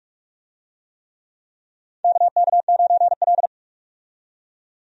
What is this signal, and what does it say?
Morse code, heard as a keyed tone, about 46 wpm, 700 Hz. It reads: KK9L
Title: USA Call Signs - 4 Characters - 45wpm